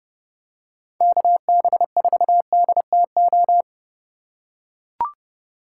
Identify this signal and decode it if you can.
Morse code, heard as a keyed tone, about 29 wpm, 700 Hz. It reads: KB4DTO